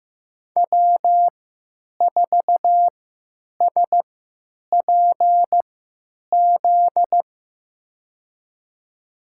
Morse code, heard as a keyed tone, about 15 wpm, spaced as for 9 wpm, 700 Hz. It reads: W4SPZ
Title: USA Call Signs - 3x Character Spacing - 15wpm